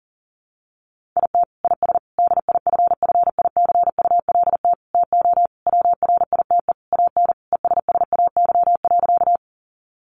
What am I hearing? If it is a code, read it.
Morse code, heard as a keyed tone, about 40 wpm, 700 Hz. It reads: IT IS DIFFICULT TO WRITE AN ESSAY.
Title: Sentences from Top 1500 Words 40wpm